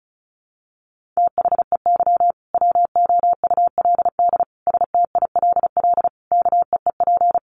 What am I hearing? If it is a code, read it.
Morse code, heard as a keyed tone, about 34 wpm, 700 Hz. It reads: THEY WOULD STILL KEEP
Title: Sentences from Top 1400 Words 35wpm